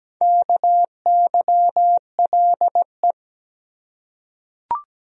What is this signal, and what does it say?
Morse code, heard as a keyed tone, about 17 wpm, 700 Hz. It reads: KYLE
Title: USA Common Names - Max 5 Letters - 17wpm